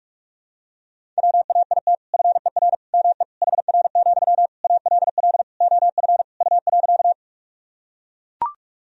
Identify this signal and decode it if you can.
Morse code, heard as a keyed tone, about 41 wpm, 700 Hz. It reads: WAIT FER ME HR, ALL OF U.
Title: Sentences from Top 500 Words and Abbreviations 45wpm